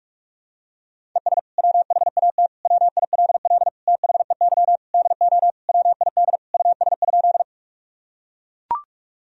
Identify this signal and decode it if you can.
Morse code, heard as a keyed tone, about 41 wpm, 700 Hz. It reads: ES WHAT WILL THEY DO WID US?
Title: Sentences from Top 200 Words and Abbreviations 45wpm